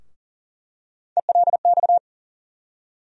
Morse code, heard as a keyed tone, about 40 wpm, 700 Hz. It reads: ELX
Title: USA Call Sign Postfixes - 3 Letters 40wpm